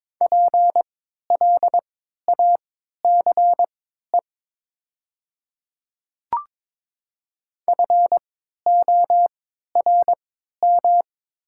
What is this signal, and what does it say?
Morse code, heard as a keyed tone, about 22 wpm, spaced as for 12 wpm, 700 Hz. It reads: PLACE FORM
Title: 3x Character Spacing - Top 200 Words - 22wpm